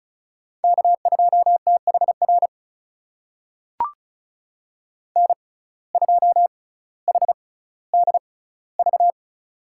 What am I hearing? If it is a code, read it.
Morse code, heard as a keyed tone, about 36 wpm, spaced as for 18 wpm, 700 Hz. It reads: K2THR N2HDV
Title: USA Call Signs - 5 Characters - 6x Character Spacing - 35wpm